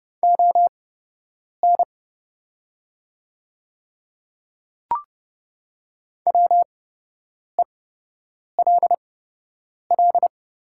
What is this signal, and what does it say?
Morse code, heard as a keyed tone, about 30 wpm, spaced as for 8 wpm, 700 Hz. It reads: ON WELL
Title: 8x Character Spacing - Top 300 Words - 30wpm